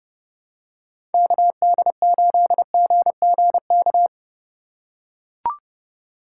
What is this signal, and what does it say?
Morse code, heard as a keyed tone, about 30 wpm, 700 Hz. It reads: KD8GGK
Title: USA Call Signs - 6 Characters - 4x Character Spacing - 30wpm